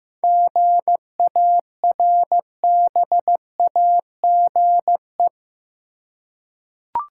Morse code, heard as a keyed tone, about 15 wpm, 700 Hz. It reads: GARBAGE
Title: Seven Letter Words - 15wpm